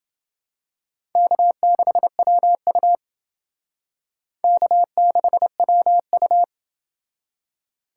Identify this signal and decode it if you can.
Morse code, heard as a keyed tone, about 28 wpm, 700 Hz. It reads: K6WU K6WU